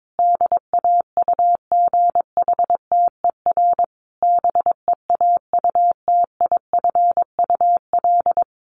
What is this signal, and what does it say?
Morse code, heard as a keyed tone, about 22 wpm, 700 Hz. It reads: DAUGHTER BEAUTIFUL
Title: Sentences from Top 1500 Words 22wpm